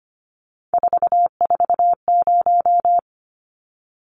Morse code, heard as a keyed tone, about 25 wpm, 700 Hz. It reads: 440
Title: Numbers - 3-Digit 25wpm